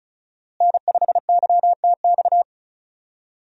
Morse code, heard as a keyed tone, about 34 wpm, 700 Hz. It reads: N5YTX
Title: USA Call Signs - 5 Characters - Repeated 3x - 35wpm